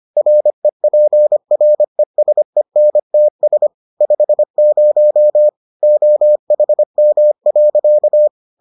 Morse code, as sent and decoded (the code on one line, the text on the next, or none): .-. .--. .-. ... -. - ... / ..... ----- / --- .... -- .-.-.-
REPRESENTS 50 OHM.